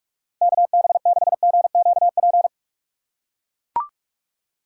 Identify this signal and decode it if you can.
Morse code, heard as a keyed tone, about 42 wpm, 700 Hz. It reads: KB6GQP
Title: USA Call Signs - 6 Characters - 3x Character Spacing - 45wpm